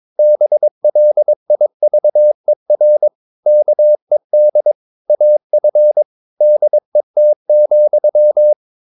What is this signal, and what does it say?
Morse code, heard as a keyed tone, about 22 wpm, 600 Hz. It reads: BLIVER KED AF DET,
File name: Ferie QRS 22wpm